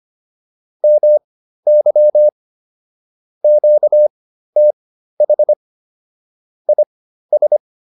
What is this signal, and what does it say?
Morse code, heard as a keyed tone, about 25 wpm, spaced as for 13 wpm, 600 Hz. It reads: MY QTH IS
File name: QSO209 13